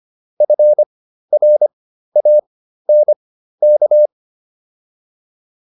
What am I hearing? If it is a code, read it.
Morse code, heard as a keyed tone, about 25 wpm, spaced as for 13 wpm, 600 Hz. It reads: FRANK